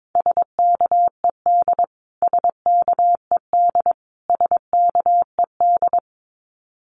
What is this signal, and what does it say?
Morse code, heard as a keyed tone, about 22 wpm, 700 Hz. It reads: SKED SKED SKED
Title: Abbreviations Repeated 3x - 22wpm